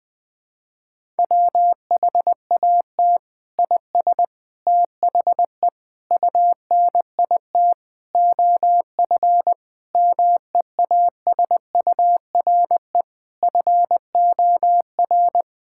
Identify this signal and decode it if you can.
Morse code, heard as a keyed tone, about 21 wpm, 700 Hz. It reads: WHAT IS THE UNIT OF MEASURE FOR
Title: Sentences from Top 1000 Words - Review Part 6 - 20wpm